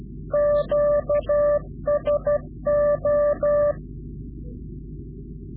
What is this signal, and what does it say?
Morse code, heard as a keyed tone, about 12 wpm, 590 Hz. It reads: QSO
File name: CW-QSO / Directional Call / Directional QSO